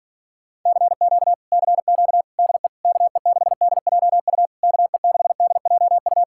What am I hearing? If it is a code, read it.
Morse code, heard as a keyed tone, about 47 wpm, 700 Hz. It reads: CQ CQ DE KE6DJU KE6DJU
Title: Speed-Racing Common QSO Phrases - 40wpm